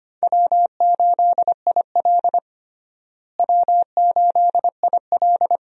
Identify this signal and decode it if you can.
Morse code, heard as a keyed tone, about 25 wpm, 700 Hz. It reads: W8IL W8IL
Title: USA Call Signs - 4 Characters - Repeated 2x - 25wpm